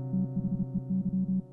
Morse code, heard as a keyed tone, about 32 wpm, 190 Hz. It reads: THEY
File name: Cooper-Moore / A Retrospective 1990-2010 / Cooper-Moore - 05 - e If Only piano and morse code